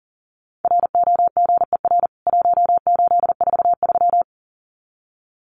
Morse code, heard as a keyed tone, about 40 wpm, 700 Hz. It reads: ROGER 1843
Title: CWT Exchange 40wpm